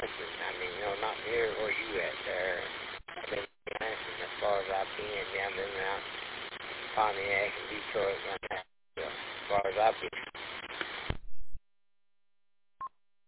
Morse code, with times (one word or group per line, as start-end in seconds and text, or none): none